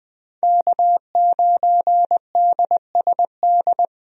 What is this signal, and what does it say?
Morse code, heard as a keyed tone, about 21 wpm, 700 Hz. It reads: K9DSD
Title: USA Call Signs - 5 Characters - Repeated 3x - 20wpm